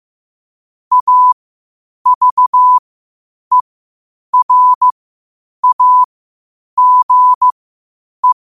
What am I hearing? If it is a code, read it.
Morse code, heard as a keyed tone, about 15 wpm, spaced as for 8 wpm, 1000 Hz. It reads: AVERAGE
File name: qotdfiles / 8wpm